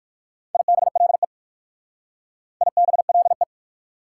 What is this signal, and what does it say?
Morse code, heard as a keyed tone, about 56 wpm, 700 Hz. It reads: IBLE IBLE